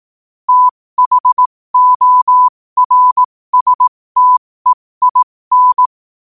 Morse code, heard as a keyed tone, about 18 wpm, spaced as for 15 wpm, 1000 Hz. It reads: THORSTEIN
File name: qotdfiles / 15wpm